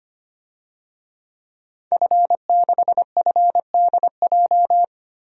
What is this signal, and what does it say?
Morse code, heard as a keyed tone, about 25 wpm, 700 Hz. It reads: F6FDJ
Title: DXCC-100 Call Signs - 5 Characters - 25wpm